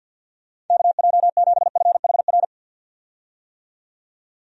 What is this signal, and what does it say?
Morse code, heard as a keyed tone, about 46 wpm, 700 Hz. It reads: KJ7FHR
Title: USA Call Signs - 6 Characters - 50wpm